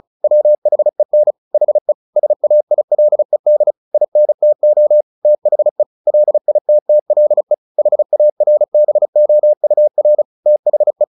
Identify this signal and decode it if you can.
Morse code, heard as a keyed tone, about 35 wpm, 600 Hz. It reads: WHEN HE SAILED INTO THE LITTLE HARBOUR THE
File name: the old man and the sea 2 35wpm